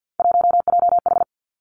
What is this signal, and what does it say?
Morse code, heard as a keyed tone, about 50 wpm, 700 Hz. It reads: 1JH
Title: Three Challenging Letter-Number Combinations 50wpm